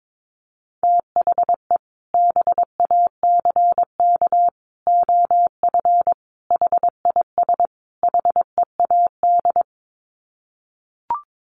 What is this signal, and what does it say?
Morse code, heard as a keyed tone, about 22 wpm, 700 Hz. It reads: THE BACK OF HIS HEAD
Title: Sets of 5 Words Encore 22wpm